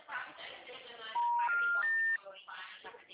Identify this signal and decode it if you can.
Special information tone, with speech.